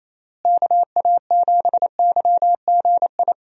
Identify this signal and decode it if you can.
Morse code, heard as a keyed tone, about 28 wpm, 700 Hz. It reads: KA7YGI
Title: USA Call Signs - 6 Characters - Repeated 3x - 28wpm